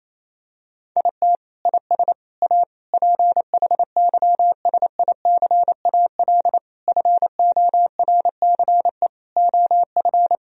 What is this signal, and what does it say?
Morse code, heard as a keyed tone, about 28 wpm, 700 Hz. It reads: IT IS A PHYSICAL FORCE OF
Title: Sentences from Top 1000 Words - Review Part 4 - 28wpm